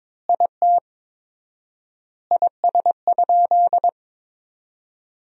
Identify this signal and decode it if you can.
Morse code, heard as a keyed tone, about 21 wpm, 700 Hz. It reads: IT IS?
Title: 4x Word-Spacing - Sentences from Top 100 Words - 22wpm